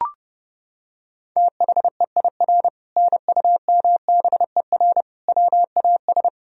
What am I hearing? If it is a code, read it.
Morse code, heard as a keyed tone, about 30 wpm, 700 Hz. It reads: THEIR NUMBER WAS
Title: Sentences from Top 1000 Words - Review Part 1 - 30wpm